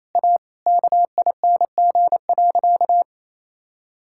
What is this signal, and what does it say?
Morse code, heard as a keyed tone, about 28 wpm, 700 Hz. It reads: A KING.
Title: Sentences from Top 1000 Words - Review Part 5 - 28wpm